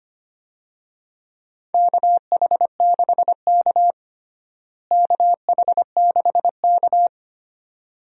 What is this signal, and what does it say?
Morse code, heard as a keyed tone, about 25 wpm, 700 Hz. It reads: KH6K KH6K
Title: DXCC-100 Call Signs - 4 Characters - Repeated 2x - 25wpm